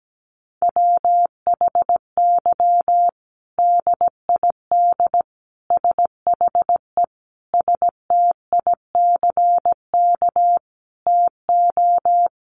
Morse code, heard as a keyed tone, about 17 wpm, 700 Hz. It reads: WHY DID SHE STICK TO
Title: Sentences from Top 1200 Words 17wpm